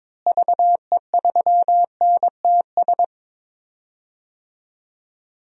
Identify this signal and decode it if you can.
Morse code, heard as a keyed tone, about 21 wpm, 700 Hz. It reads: VE3NTS
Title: DXCC-100 Call Signs - 6 Characters - Repeated 3x - 22wpm